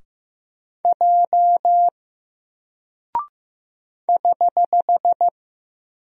Morse code, heard as a keyed tone, about 15 wpm, 700 Hz. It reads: J <HH>
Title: Single Character (Letter:Number:Punctuation:Prosign) Alt 15wpm